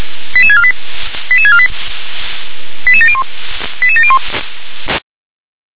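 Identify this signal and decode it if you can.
Five-tone selective call (ZVEI), heard as quick runs of five tones, tones of about 70 ms. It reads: 88648 88648 88701 88701
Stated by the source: ZVEI five-tone call 88149 received by SDR on channel 357.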